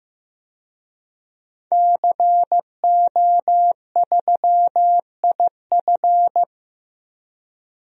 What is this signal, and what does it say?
Morse code, heard as a keyed tone, about 15 wpm, 700 Hz. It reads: CO3IF
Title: DXCC-100 Call Signs - 5 Characters - 15wpm